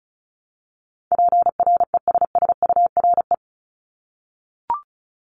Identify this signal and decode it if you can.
Morse code, heard as a keyed tone, about 36 wpm, 700 Hz. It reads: PRESSURE
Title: Multi-letter Phonemes Repeated 3x 35wpm